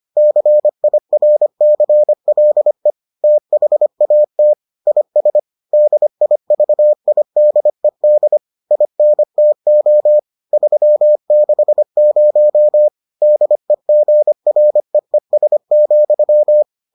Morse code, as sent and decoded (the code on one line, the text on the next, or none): -.-. .. .-. -.-. .-.. / - .... .- - / .. ... / -.. .. ...- .. -.. -.. / .. -. - --- / ...-- -.... ----- / -.. --. .-. . ... --..--
CIRCLE THAT IS DIVIDED INTO 360 DEGREES,